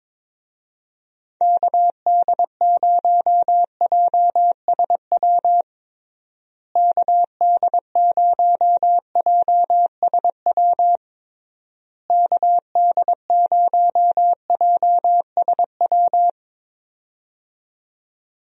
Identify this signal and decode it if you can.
Morse code, heard as a keyed tone, about 22 wpm, 700 Hz. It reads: KD0JSW KD0JSW KD0JSW
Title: USA Call Signs - 6 Characters - Repeated 3x - 22wpm